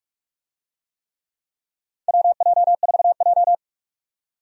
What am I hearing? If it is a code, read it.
Morse code, heard as a keyed tone, about 41 wpm, 700 Hz. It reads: WJ4J
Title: USA Call Signs - 4 Characters - 45wpm